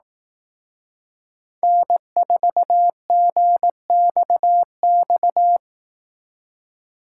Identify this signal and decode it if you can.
Morse code, heard as a keyed tone, about 18 wpm, 700 Hz. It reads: N4GXX
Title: Speed Racing - USA 1x3 Call Signs 15wpm